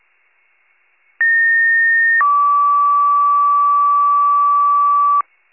Two-tone page, heard as a one-second tone, then a three-second tone, about 1850 then 1150 Hz. It is heard from 1.2 to 5.2 s.